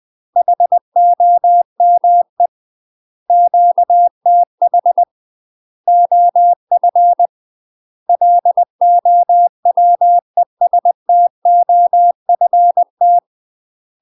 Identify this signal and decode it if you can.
Morse code, heard as a keyed tone, about 20 wpm, 700 Hz. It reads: HOME QTH OF LOWESTOFT
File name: UK MASTER QSOs TRAINING / 20WPM / UK MASTER QSOs 0012